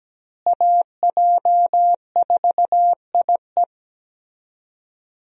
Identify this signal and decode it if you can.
Morse code, heard as a keyed tone, about 17 wpm, 700 Hz. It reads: AJ4IE
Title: USA Call Signs - 5 Characters - Repeated 3x - 17wpm